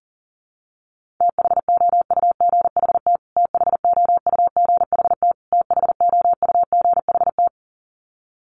Morse code, heard as a keyed tone, about 40 wpm, 700 Hz. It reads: THOUGHT THOUGHT THOUGHT
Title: Top 500 Words Repeated 40wpm